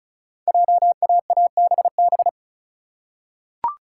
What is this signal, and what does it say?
Morse code, heard as a keyed tone, about 36 wpm, 690 Hz. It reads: JAABB
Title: Code Groups - 5 Letters - 35wpm